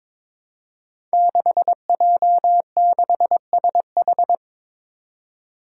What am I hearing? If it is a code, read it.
Morse code, heard as a keyed tone, about 22 wpm, 700 Hz. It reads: 6J6SH
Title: Five Challenging Letter-Number Combinations 22wpm